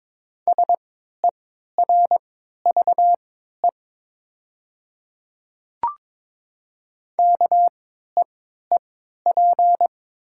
Morse code, heard as a keyed tone, about 22 wpm, spaced as for 12 wpm, 700 Hz. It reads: SERVE KEEP
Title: 3x Character Spacing - Top 500 Words - 22wpm